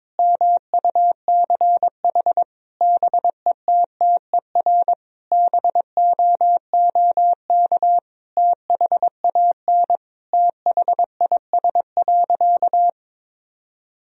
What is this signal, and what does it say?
Morse code, heard as a keyed tone, about 22 wpm, 700 Hz. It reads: MUCH BETTER BOOK THAN THIS.
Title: Sentences from Top 1000 Words - Review Part 3 - 22wpm